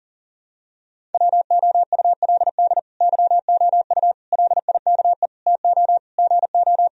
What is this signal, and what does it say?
Morse code, heard as a keyed tone, about 40 wpm, 700 Hz. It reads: WOULD YOU LIKE TO GO
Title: Sentences from Top 1000 Words - Review Part 1 - 40wpm